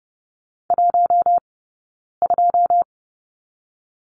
Morse code, heard as a keyed tone, about 30 wpm, 700 Hz. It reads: 1 2
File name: Lesson 17d - Letter u - Two Characters 30wpm